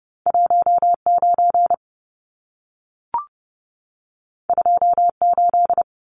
Morse code, heard as a keyed tone, about 30 wpm, 700 Hz. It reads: 19 28
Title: Numbers - 2-Digit 30wpm